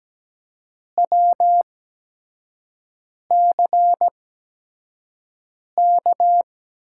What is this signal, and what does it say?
Morse code, heard as a keyed tone, about 17 wpm, spaced as for 5 wpm, 700 Hz. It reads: WCK
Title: USA Call Signs - 8x Character Spacing - 17wpm